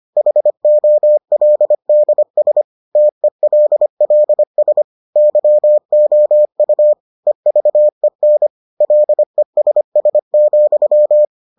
Morse code, heard as a keyed tone, about 25 wpm, 600 Hz. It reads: HOLDS TELLS YOU EVEN LESS,